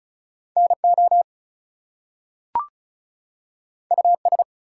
Morse code, heard as a keyed tone, about 36 wpm, 700 Hz. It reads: NO US
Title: Two Letter Words - 35wpm